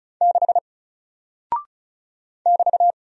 Morse code, heard as a keyed tone, about 36 wpm, 700 Hz. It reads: 6 <BT>
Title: Single Character (Letter:Number:Punctuation:Prosign) Alt 35wpm